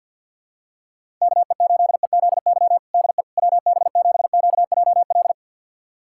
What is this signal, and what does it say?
Morse code, heard as a keyed tone, about 50 wpm, 690 Hz. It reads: KE8EZY DE WB7QJL
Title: USA Callsign DE Callsign - 6 Characters - 50wpm